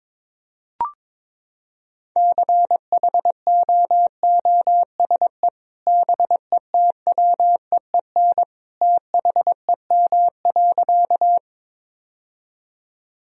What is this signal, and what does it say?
Morse code, heard as a keyed tone, about 22 wpm, 700 Hz. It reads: CHOOSE BETWEEN THEM.
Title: Sentences from Top 1000 Words - Review Part 4 - 22wpm